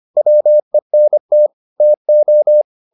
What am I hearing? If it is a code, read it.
Morse code, heard as a keyed tone, about 25 wpm, 600 Hz. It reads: WENT TO
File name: the old man and the sea part 1 25wpm